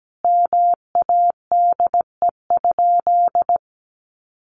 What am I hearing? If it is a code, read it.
Morse code, heard as a keyed tone, about 17 wpm, 700 Hz. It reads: MADE?
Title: Sentences from Top 1400 Words 17wpm